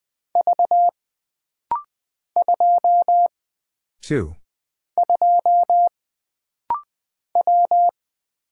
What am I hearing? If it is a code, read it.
Morse code, heard as a keyed tone, about 20 wpm, 700 Hz. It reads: V 2 2 W
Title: Single Character (Letter:Number:Punctuation:Prosign) - Rapid-Fire - 20wpm